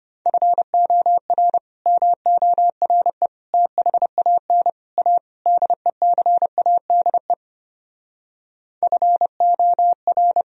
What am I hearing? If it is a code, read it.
Morse code, heard as a keyed tone, about 33 wpm, 700 Hz. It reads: FOR MORE THAN A DECADE FOR